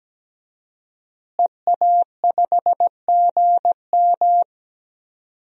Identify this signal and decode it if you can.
Morse code, heard as a keyed tone, about 17 wpm, 700 Hz. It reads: EA5GM
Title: DXCC-100 Call Signs - 5 Characters - Repeated 2x - 17wpm